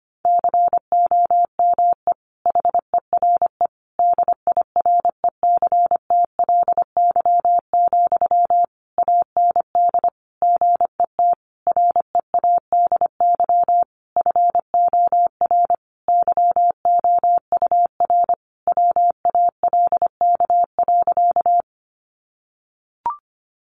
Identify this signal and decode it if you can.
Morse code, heard as a keyed tone, about 25 wpm, 700 Hz. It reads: COME HERE DIRECTLY, AND GET READY FOR YOUR WALK.
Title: Sentences from Top 800 Words 25wpm